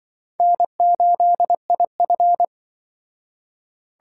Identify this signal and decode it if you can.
Morse code, heard as a keyed tone, about 24 wpm, 700 Hz. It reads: N8IF
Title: Speed Racing - USA 1x2 Call Signs 20wpm